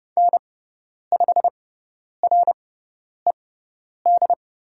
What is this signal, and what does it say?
Morse code, heard as a keyed tone, about 30 wpm, spaced as for 10 wpm, 700 Hz. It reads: N5RED
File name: Lesson 17g - Letter u - Call Signs 30wpm